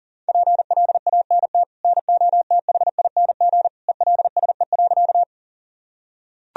Morse code, heard as a keyed tone, about 40 wpm, 700 Hz. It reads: PLANT NOTHING ELSE.
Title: Sentences from Top 1000 Words - Review Part 1 - 40wpm